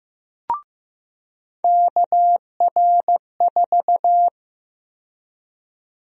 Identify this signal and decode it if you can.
Morse code, heard as a keyed tone, about 16 wpm, 700 Hz. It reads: KR4
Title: USA Call Sign Prefixes - 15wpm